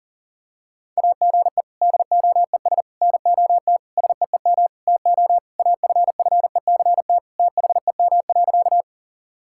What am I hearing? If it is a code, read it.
Morse code, heard as a keyed tone, about 40 wpm, 700 Hz. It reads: AGE DOES NOT SEEM TO AFFECT THEM.
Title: Sentences from Top 1000 Words - Review Part 6 - 40wpm